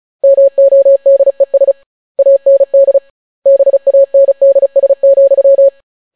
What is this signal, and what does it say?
Morse code, heard as a keyed tone, about 36 wpm, 550 Hz. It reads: MODES AND BANDS,